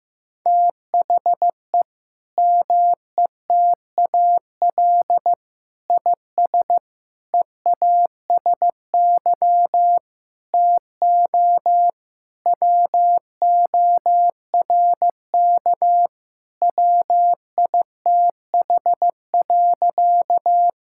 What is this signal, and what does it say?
Morse code, heard as a keyed tone, about 15 wpm, 700 Hz. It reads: THE METAL IS EASY TO WORK WITH.
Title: Sentences from Top 1000 Words - Review Part 6 - 15wpm